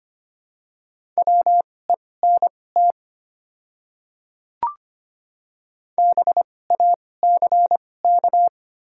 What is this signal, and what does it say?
Morse code, heard as a keyed tone, about 25 wpm, spaced as for 18 wpm, 690 Hz. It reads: WENT BACK
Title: 2x Character Spacing - Top 200 Words - 25wpm